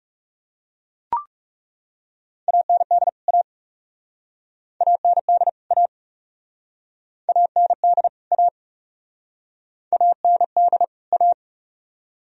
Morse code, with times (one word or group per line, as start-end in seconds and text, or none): 2.5-3.1 AND
3.3-3.4 A
4.8-5.5 AND
5.7-5.9 A
7.3-8.1 AND
8.3-8.5 A
9.9-10.8 AND
11.1-11.3 A